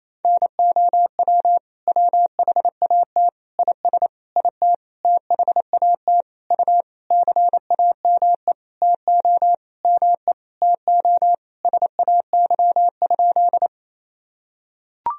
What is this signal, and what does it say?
Morse code, heard as a keyed tone, about 28 wpm, 700 Hz. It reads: NOW WHAT IS IT THAT U CAME TO ME TO SAY?
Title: Sentences from Top 200 Words and Abbreviations 28wpm